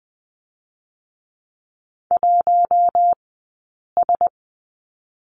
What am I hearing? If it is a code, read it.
Morse code, heard as a keyed tone, about 20 wpm, 700 Hz. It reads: 1 S
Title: 073 Lesson 13d - Letter L - Two Characters 20wpm